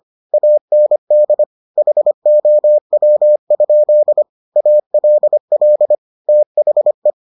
Morse code, heard as a keyed tone, about 25 wpm, 600 Hz. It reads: AND HOW? ALL THE